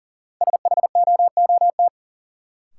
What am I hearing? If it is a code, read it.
Morse code, heard as a keyed tone, about 40 wpm, 700 Hz. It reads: SHOOT
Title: Sentences from Top 1200 Words - New Word Drill Alt - 40wpm